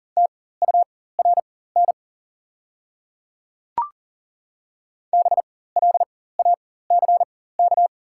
Morse code, heard as a keyed tone, about 40 wpm, spaced as for 19 wpm, 690 Hz. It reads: TURN BLACK
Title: 4x Character Spacing - Top 500 Words - 40wpm